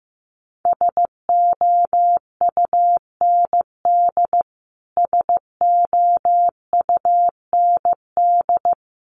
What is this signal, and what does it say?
Morse code, heard as a keyed tone, about 15 wpm, 700 Hz. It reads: SOUND SOUND
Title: Top 200 Words Repeated 15wpm